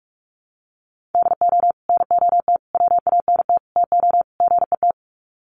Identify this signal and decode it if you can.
Morse code, heard as a keyed tone, about 44 wpm, 700 Hz. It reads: DO NOT WANT TO GET
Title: Sets of 5 Words 45wpm